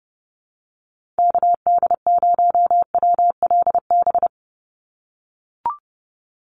Morse code, heard as a keyed tone, about 30 wpm, 700 Hz. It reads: KD0WLB
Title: USA Call Signs 30wpm